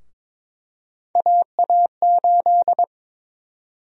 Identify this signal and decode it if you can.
Morse code, heard as a keyed tone, about 22 wpm, 700 Hz. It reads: AA8
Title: USA Call Sign Prefixes - Repeated 2x - 22wpm